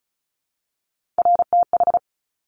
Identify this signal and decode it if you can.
Morse code, heard as a keyed tone, about 36 wpm, 700 Hz. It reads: RTH
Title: Top Three Letter Combinations in Words 35wpm